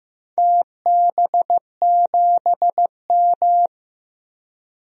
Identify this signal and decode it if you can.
Morse code, heard as a keyed tone, about 15 wpm, 700 Hz. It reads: TB7M
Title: DXCC-100 Call Signs - 4 Characters - 15wpm